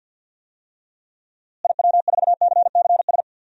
Morse code, heard as a keyed tone, about 52 wpm, 700 Hz. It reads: IW4CCS
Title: DXCC-100 Call Signs - 6 Characters - Repeated 3x - 50wpm